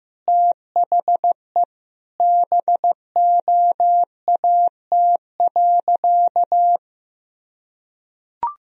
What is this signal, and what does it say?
Morse code, heard as a keyed tone, about 15 wpm, 700 Hz. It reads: THE BOAT.
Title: Sentences from Top 1000 Words - Review Part 3 - 15wpm